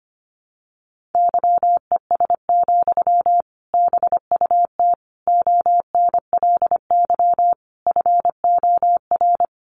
Morse code, heard as a keyed tone, about 25 wpm, 700 Hz. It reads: YES, BUT ONLY FOR
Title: Sentences from Top 500 Words 25wpm